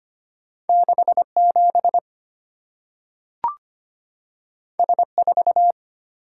Morse code, heard as a keyed tone, about 25 wpm, 700 Hz. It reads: 6 7 S 4
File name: Two Letter-Number Combinations - 25wpm